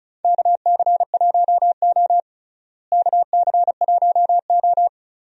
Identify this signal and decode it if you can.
Morse code, heard as a keyed tone, about 34 wpm, 700 Hz. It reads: KC1O KC1O
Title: USA Call Signs - 4 Characters - Repeated 2x - 35wpm